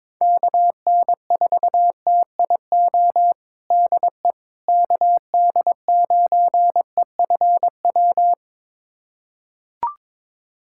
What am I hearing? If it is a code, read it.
Morse code, heard as a keyed tone, about 22 wpm, 700 Hz. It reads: KN4TIO DE KD9EFW
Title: USA Callsign DE Callsign - 6 Characters - 22wpm